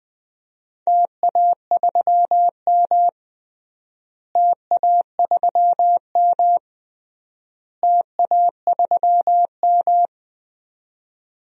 Morse code, heard as a keyed tone, about 20 wpm, 700 Hz. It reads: TA3M TA3M TA3M
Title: DXCC-100 Call Signs - 4 Characters - Repeated 3x - 20wpm